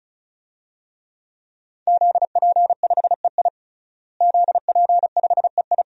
Morse code, heard as a keyed tone, about 34 wpm, 700 Hz. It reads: ZP5EI ZP5EI
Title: DXCC-100 Call Signs - 5 Characters - Repeated 2x - 35wpm